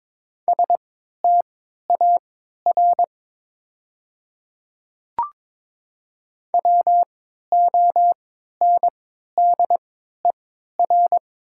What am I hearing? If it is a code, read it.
Morse code, heard as a keyed tone, about 22 wpm, spaced as for 12 wpm, 700 Hz. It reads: STAR WONDER
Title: 3x Character Spacing - Top 500 Words - 22wpm